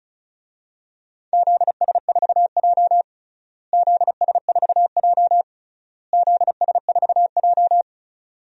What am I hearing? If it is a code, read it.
Morse code, heard as a keyed tone, about 36 wpm, 700 Hz. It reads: ZS4J ZS4J ZS4J
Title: DXCC-100 Call Signs - 4 Characters - Repeated 3x - 35wpm